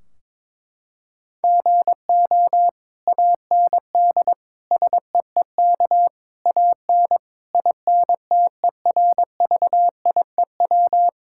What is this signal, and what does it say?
Morse code, heard as a keyed tone, about 22 wpm, 700 Hz. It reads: GO AND SEEK AN INTERVIEW
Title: Sentences from Top 1600 Words 22wpm